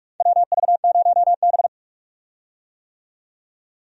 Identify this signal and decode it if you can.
Morse code, heard as a keyed tone, about 46 wpm, 700 Hz. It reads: WV0B
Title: USA Call Signs - 4 Characters - 45wpm